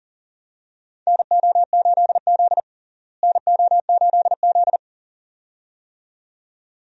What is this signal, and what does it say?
Morse code, heard as a keyed tone, about 40 wpm, 700 Hz. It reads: NO8Z NO8Z
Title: USA Call Signs - 4 Characters - Repeated 2x - 40wpm